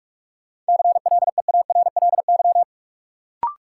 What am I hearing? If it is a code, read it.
Morse code, heard as a keyed tone, about 47 wpm, 700 Hz. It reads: CLEARLY
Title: Sentences from Top 1300 Words - New Word Drill - 45wpm